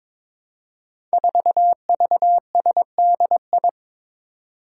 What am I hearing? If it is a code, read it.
Morse code, heard as a keyed tone, about 22 wpm, 700 Hz. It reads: 4VSDI
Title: Five Challenging Letter-Number Combinations 22wpm